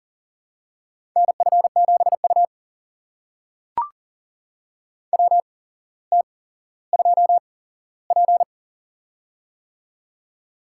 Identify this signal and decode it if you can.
Morse code, heard as a keyed tone, about 40 wpm, spaced as for 17 wpm, 700 Hz. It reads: NF7U WT2P
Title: USA Call Signs - 4 Characters - 8x Character Spacing - 40wpm